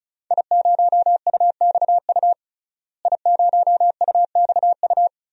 Morse code, heard as a keyed tone, about 36 wpm, 700 Hz. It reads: I0UXU I0UXU
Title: DXCC-100 Call Signs - 5 Characters - Repeated 2x - 35wpm